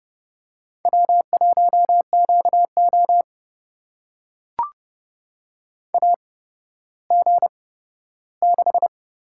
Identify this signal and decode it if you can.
Morse code, heard as a keyed tone, about 30 wpm, spaced as for 14 wpm, 700 Hz. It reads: W1QO AG6
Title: USA Call Signs - 4 Characters - 8x Character Spacing - 30wpm